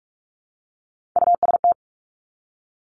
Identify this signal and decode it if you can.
Morse code, heard as a keyed tone, about 46 wpm, 700 Hz. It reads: UST
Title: Top Three Letter Combinations in Words 45wpm